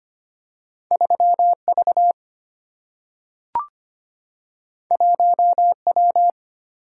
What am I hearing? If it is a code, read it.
Morse code, heard as a keyed tone, about 25 wpm, 700 Hz. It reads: 3V 1W
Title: Single-Letter Single-Number 25wpm